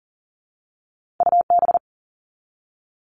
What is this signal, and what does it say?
Morse code, heard as a keyed tone, about 42 wpm, 700 Hz. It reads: U B
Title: All Cut Numbers - 2-Digit 40wpm